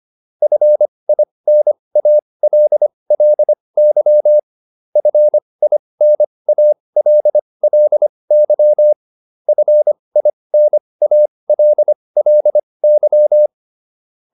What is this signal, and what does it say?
Morse code, heard as a keyed tone, about 25 wpm, spaced as for 20 wpm, 600 Hz. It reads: FINALLY FINALLY FINALLY